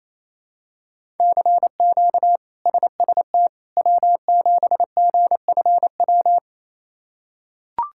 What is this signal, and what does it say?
Morse code, heard as a keyed tone, about 28 wpm, 700 Hz. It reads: CQ SST W7GFW
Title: K1USN SST QSO Practice - 28wpm